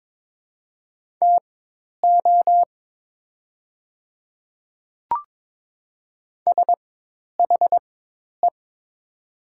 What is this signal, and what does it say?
Morse code, heard as a keyed tone, about 22 wpm, spaced as for 10 wpm, 700 Hz. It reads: TO SHE
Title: 4x Character Spacing - Top 100 Words - 22wpm